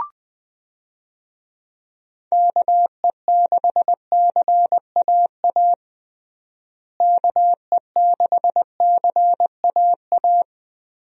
Morse code, heard as a keyed tone, about 20 wpm, 700 Hz. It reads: KE6CAA KE6CAA
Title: USA Call Signs - 6 Characters - Repeated 3x - 20wpm